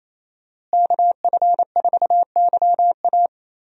Morse code, heard as a keyed tone, about 28 wpm, 700 Hz. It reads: KF4YA
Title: USA Call Signs - 5 Characters - Repeated 3x - 28wpm